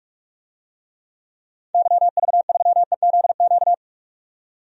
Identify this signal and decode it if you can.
Morse code, heard as a keyed tone, about 46 wpm, 700 Hz. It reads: YV3EZQ
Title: DXCC-100 Call Signs - 6 Characters - 45wpm